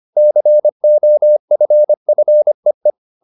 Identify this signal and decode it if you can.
Morse code, heard as a keyed tone, about 25 wpm, 600 Hz. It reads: COFFEE
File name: the old man and the sea part 1 25wpm